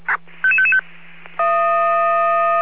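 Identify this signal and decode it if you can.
Five-tone selective call (ZVEI), heard as quick runs of five tones, tones of about 70 ms. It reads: 55555
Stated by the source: ZVEI five-tone call received on SDR.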